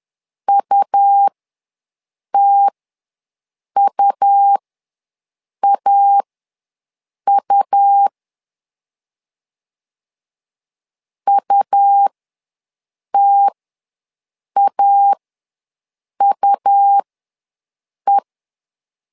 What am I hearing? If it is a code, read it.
Morse code, heard as a keyed tone, about 11 wpm, spaced as for 6 wpm, 800 Hz. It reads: UTUAU UTAUE
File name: LEZIONI / 2 UHDN / 2 U ws 10wpm 2017